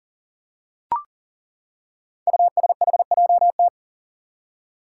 Morse code, heard as a keyed tone, about 42 wpm, 690 Hz. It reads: USHJT
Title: Five Challenging Letter-Number Combinations 40wpm